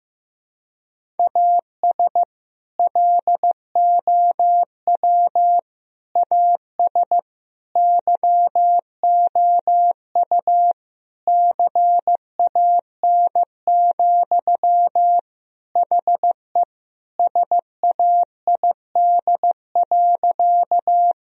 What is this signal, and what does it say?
Morse code, heard as a keyed tone, about 15 wpm, 700 Hz. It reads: AS LOW AS YOU CAN, HE SAID.
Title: Sentences from Top 1000 Words - Review Part 1 - 15wpm